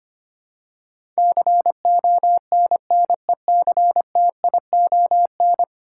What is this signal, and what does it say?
Morse code, heard as a keyed tone, about 25 wpm, 700 Hz. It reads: CONNECTION
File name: Ten Letter Words - 25wpm